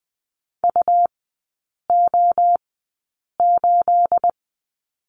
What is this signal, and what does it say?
Morse code, heard as a keyed tone, about 20 wpm, spaced as for 8 wpm, 690 Hz. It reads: UO8
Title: 212 Lesson 34e - Number 8 - Three Characters 20wpm